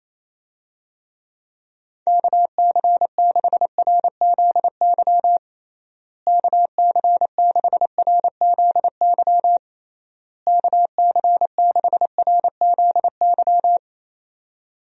Morse code, heard as a keyed tone, about 27 wpm, 700 Hz. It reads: KC6RZY KC6RZY KC6RZY
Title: USA Call Signs - 6 Characters - Repeated 3x - 28wpm